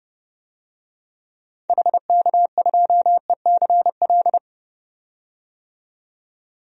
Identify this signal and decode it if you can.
Morse code, heard as a keyed tone, about 32 wpm, 700 Hz. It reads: HK2ECL
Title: DXCC-100 Call Signs - 6 Characters - 30wpm